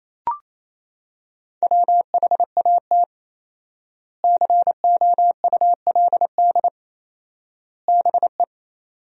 Morse code, heard as a keyed tone, about 27 wpm, 700 Hz. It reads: WHAT COULD BE
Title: 4x Word-Spacing - Sentences from Top 400 Words - 28wpm